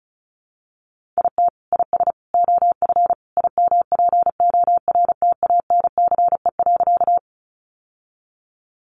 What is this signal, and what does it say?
Morse code, heard as a keyed tone, about 34 wpm, 700 Hz. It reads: IT IS OF IMPORTANCE.
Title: Sentences from Top 1200 Words 35wpm